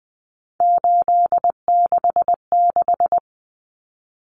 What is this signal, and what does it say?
Morse code, heard as a keyed tone, about 20 wpm, 700 Hz. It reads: 866
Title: Common Cut Numbers - 3-Digit 20wpm